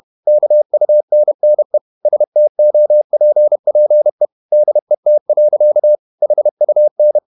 Morse code, heard as a keyed tone, about 31 wpm, 600 Hz. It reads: KUNNE STOPPE DET. HUN
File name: Ferie QRQ 31wpm